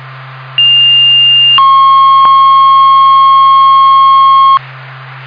Two-tone page, heard as a one-second tone, then a three-second tone, about 2800 then 1080 Hz. It is heard from 0.6 to 4.6 s.